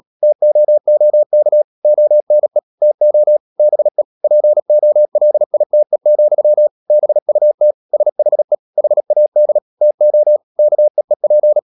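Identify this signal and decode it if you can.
Morse code, heard as a keyed tone, about 37 wpm, 600 Hz. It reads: TOOK ONE TO BE POLITE, BUT SHE HAD TO KEEP
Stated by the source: Morse code practice at 35–39 wpm sending English text of Ken Follett's novel Never, from CW med Gnister.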